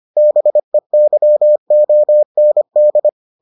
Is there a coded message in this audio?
Yes: Morse code from 0.2 to 3.1 s, heard as a keyed tone, about 25 wpm, 600 Hz. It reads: BEYOND